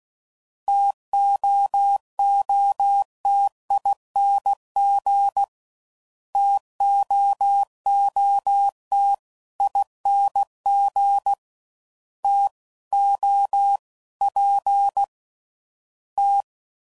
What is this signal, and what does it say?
Morse code, heard as a keyed tone, about 16 wpm, 800 Hz. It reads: TOOTING TOOT ING T O P T